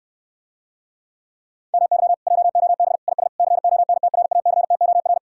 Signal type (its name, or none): Morse code